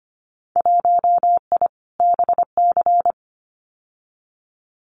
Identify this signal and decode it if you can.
Morse code, heard as a keyed tone, about 25 wpm, 700 Hz. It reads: 1I BC
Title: Winter Field Day Exchanges Encore 25wpm